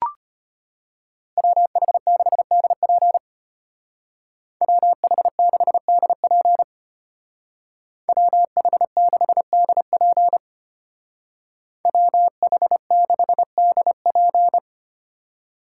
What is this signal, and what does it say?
Morse code, heard as a keyed tone, about 31 wpm, 700 Hz. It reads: WH6DP WH6DP WH6DP WH6DP
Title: Speed Racing - USA 2x2 Call Signs 25wpm